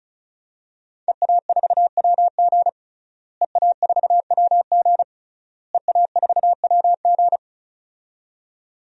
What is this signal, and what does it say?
Morse code, heard as a keyed tone, about 34 wpm, 700 Hz. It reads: EA4WG EA4WG EA4WG
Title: DXCC-100 Call Signs - 5 Characters - Repeated 3x - 35wpm